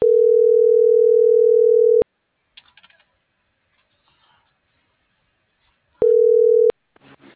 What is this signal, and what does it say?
Ringback tone (North America), 440 + 480 Hz.